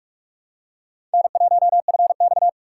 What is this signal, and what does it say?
Morse code, heard as a keyed tone, about 46 wpm, 700 Hz. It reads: N1FX
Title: USA Call Signs - 4 Characters - 6x Character Spacing - 45wpm